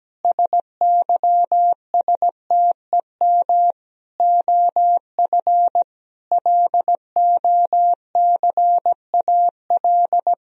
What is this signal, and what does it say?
Morse code, heard as a keyed tone, about 17 wpm, 700 Hz. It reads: SYSTEM OF LOCAL
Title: Sentences from Top 1000 Words - Review Part 3 - 17wpm